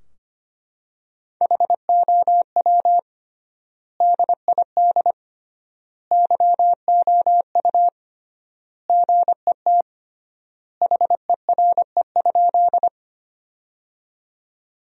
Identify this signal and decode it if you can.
Morse code, heard as a keyed tone, about 25 wpm, 700 Hz. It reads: HOW DID YOU GET HERE?
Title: 3x Word-Spacing - Sentences from Top 200 Words - 25wpm